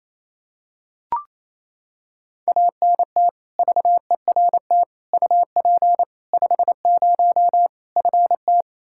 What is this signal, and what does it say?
Morse code, heard as a keyed tone, about 28 wpm, 700 Hz. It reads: ANT VERT UP 50 FT
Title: Practice QSO - Exchange 2 - Info - 28wpm